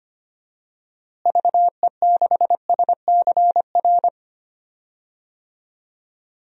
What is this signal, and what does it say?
Morse code, heard as a keyed tone, about 25 wpm, 700 Hz. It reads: VE6SCR
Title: DXCC-100 Call Signs - 6 Characters - Repeated 3x - 25wpm